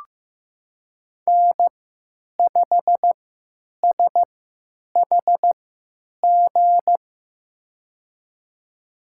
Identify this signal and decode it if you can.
Morse code, heard as a keyed tone, about 15 wpm, spaced as for 9 wpm, 700 Hz. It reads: N5SHG
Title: USA Call Signs - 5 Characters - 3x Character Spacing - 15wpm